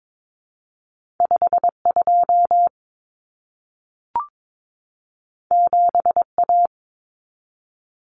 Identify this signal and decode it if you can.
Morse code, heard as a keyed tone, about 22 wpm, 700 Hz. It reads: 52 7A
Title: Common Cut Numbers - 2-Digit 22wpm